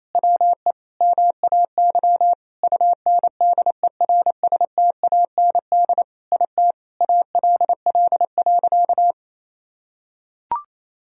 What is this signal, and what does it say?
Morse code, heard as a keyed tone, about 28 wpm, 700 Hz. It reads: WE MAY UNDERSTAND IT ALL.
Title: Sentences from Top 1000 Words - Review Part 4 - 28wpm